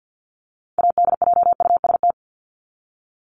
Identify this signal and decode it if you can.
Morse code, heard as a keyed tone, about 50 wpm, 700 Hz. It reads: ADJUST
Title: Top 1501-1750 Words 50wpm